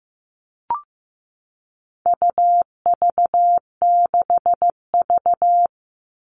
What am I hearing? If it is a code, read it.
Morse code, heard as a keyed tone, about 15 wpm, 700 Hz. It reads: UV6V
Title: All Cut Numbers - 4-Digit 15wpm